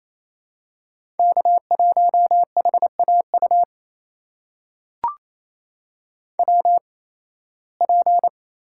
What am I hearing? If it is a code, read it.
Morse code, heard as a keyed tone, about 28 wpm, spaced as for 18 wpm, 700 Hz. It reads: K1HAU WP
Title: USA Call Signs - 8x Character Spacing - 28wpm